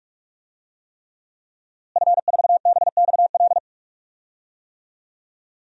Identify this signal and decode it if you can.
Morse code, heard as a keyed tone, about 44 wpm, 700 Hz. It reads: F4BXL
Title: DXCC-100 Call Signs - 5 Characters - Repeated 3x - 45wpm